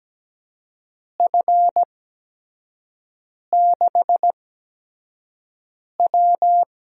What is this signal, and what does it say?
Morse code, heard as a keyed tone, about 17 wpm, spaced as for 5 wpm, 700 Hz. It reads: F6W
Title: USA Call Signs - 4 Characters - 8x Character Spacing - 17wpm